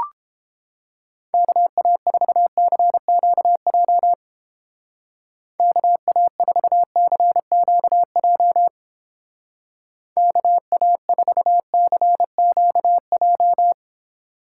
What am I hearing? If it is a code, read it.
Morse code, heard as a keyed tone, about 29 wpm, 700 Hz. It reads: KA4CQJ KA4CQJ KA4CQJ